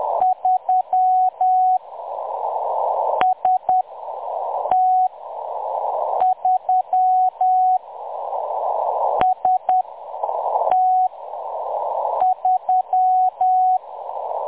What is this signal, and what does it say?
Morse code, heard as a keyed tone, about 10 wpm, spaced as for 6 wpm, 740 Hz. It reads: 3 ST3 ST3
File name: M23-ST3- -0858utc-10184khz